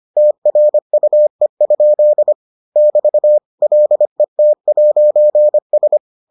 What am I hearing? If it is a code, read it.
Morse code, heard as a keyed tone, about 25 wpm, 600 Hz. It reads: TRUE? <BT> LET'S